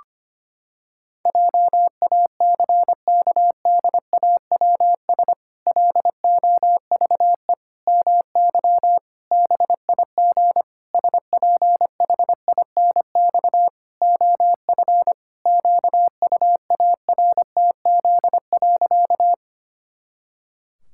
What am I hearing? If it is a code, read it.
Morse code, heard as a keyed tone, about 25 wpm, 700 Hz. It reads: JACKDAWS LOVE MY BIG SPHINX OF QUARTZ.
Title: Pangram Challenge v2 - 25wpm